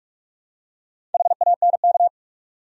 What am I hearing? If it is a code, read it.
Morse code, heard as a keyed tone, about 44 wpm, 700 Hz. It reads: HANK
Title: USA Common Names - Max 4 Letters - 45wpm